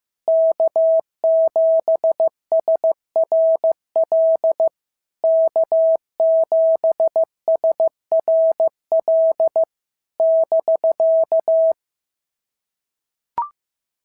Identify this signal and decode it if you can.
Morse code, heard as a keyed tone, about 14 wpm, 650 Hz. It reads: K7SRL K7SRL <BK>
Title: SOTA QSO Practice 15wpm